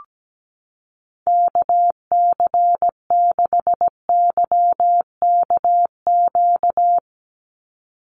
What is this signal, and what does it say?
Morse code, heard as a keyed tone, about 17 wpm, 700 Hz. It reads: KC6YKQ
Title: USA Call Signs 17wpm